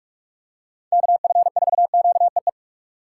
Morse code, heard as a keyed tone, about 46 wpm, 700 Hz. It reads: KF4QEE
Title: USA Call Signs - 6 Characters - 6x Character Spacing - 45wpm